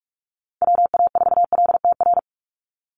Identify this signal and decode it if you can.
Morse code, heard as a keyed tone, about 44 wpm, 700 Hz. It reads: WA4LTR
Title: USA Call Signs 45wpm